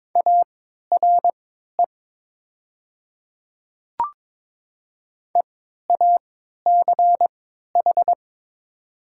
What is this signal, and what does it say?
Morse code, heard as a keyed tone, about 22 wpm, spaced as for 13 wpm, 700 Hz. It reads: ARE EACH
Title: 3x Character Spacing - Top 100 Words - 22wpm